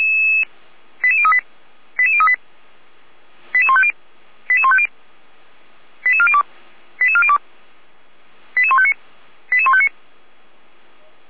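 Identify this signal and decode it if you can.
Five-tone selective call (ZVEI), heard as quick runs of five tones, tones of about 70 ms. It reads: 88038 88038 88150 88150 88402 88402 88159 88159
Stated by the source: ZVEI five-tone call received by SDR.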